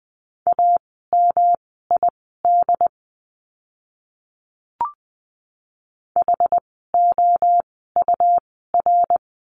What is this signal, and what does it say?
Morse code, heard as a keyed tone, about 20 wpm, spaced as for 15 wpm, 700 Hz. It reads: AMID HOUR
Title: 108 Lesson 18f - Letter m - Words 20wpm